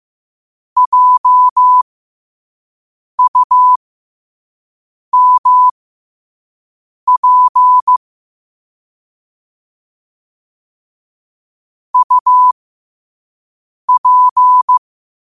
Morse code, heard as a keyed tone, about 15 wpm, spaced as for 5 wpm, 1000 Hz. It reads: JUMP UP
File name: qotdfiles / 5wpm